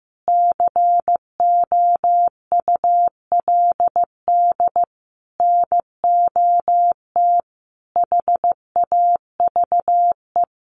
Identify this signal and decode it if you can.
Morse code, heard as a keyed tone, about 15 wpm, 700 Hz. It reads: COULD NOT HAVE
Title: Sentences from Top 1100 Words 15wpm